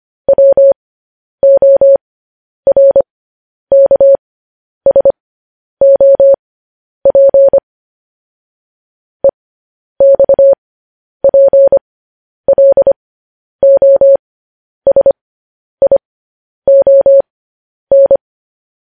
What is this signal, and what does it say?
Morse code, heard as a keyed tone, about 25 wpm, spaced as for 10 wpm, 550 Hz. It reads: WORKSOP EXPLOSION
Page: news headlines in morse code at 10 wpm